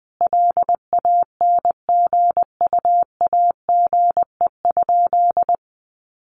Morse code, heard as a keyed tone, about 20 wpm, 700 Hz. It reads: LANGUAGE?
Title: Sentences from Top 900 Words 20wpm